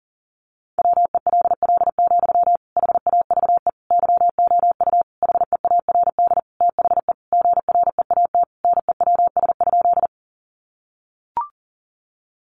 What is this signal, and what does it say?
Morse code, heard as a keyed tone, about 40 wpm, 700 Hz. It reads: WELL, HAVE YOU HEARD THE GREAT NEWS?
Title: Sentences from Top 800 Words 40wpm